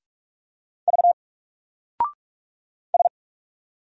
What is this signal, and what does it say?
Morse code, heard as a keyed tone, about 46 wpm, 700 Hz. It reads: V S
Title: Single Character (Letter:Number:Punctuation:Prosign) 45wpm